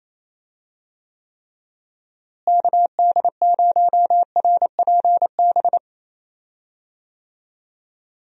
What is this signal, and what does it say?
Morse code, heard as a keyed tone, about 28 wpm, 700 Hz. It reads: KD0RPB
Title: USA Call Signs - 6 Characters - Repeated 3x - 28wpm